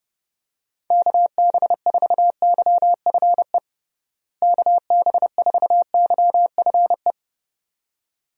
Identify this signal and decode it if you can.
Morse code, heard as a keyed tone, about 32 wpm, 700 Hz. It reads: KB4YFE KB4YFE
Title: USA Call Signs - 6 Characters - Repeated 2x - 30wpm